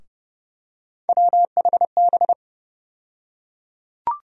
Morse code, heard as a keyed tone, about 30 wpm, 700 Hz. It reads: WHB
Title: USA Call Sign Postfixes - 3 Letters 30wpm